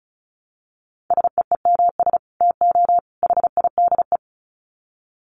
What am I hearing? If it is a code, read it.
Morse code, heard as a keyed tone, about 34 wpm, 700 Hz. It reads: SEEMS TO HIDE
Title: Dit Challenge 35wpm